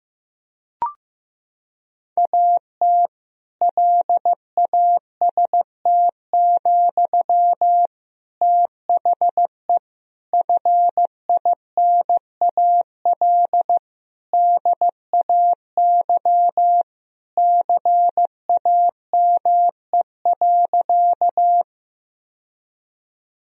Morse code, heard as a keyed tone, about 15 wpm, 700 Hz. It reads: AT LAST, THE FINAL DAY CAME.
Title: Sentences from Top 1000 Words - Review Part 2 - 15wpm